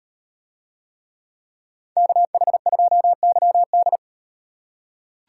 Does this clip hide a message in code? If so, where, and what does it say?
Yes: Morse code from 2.0 to 4.0 s, heard as a keyed tone, about 38 wpm, 700 Hz. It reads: KH2YD